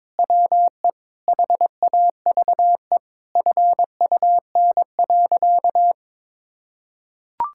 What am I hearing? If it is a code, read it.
Morse code, heard as a keyed tone, about 22 wpm, 700 Hz. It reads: WE HAVE FUN.
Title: Sentences from Top 1000 Words - Review Part 5 - 22wpm